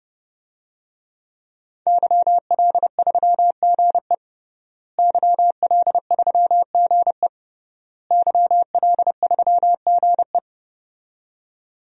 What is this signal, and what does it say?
Morse code, heard as a keyed tone, about 31 wpm, 700 Hz. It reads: YL3GE YL3GE YL3GE
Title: DXCC-100 Call Signs - 5 Characters - Repeated 3x - 30wpm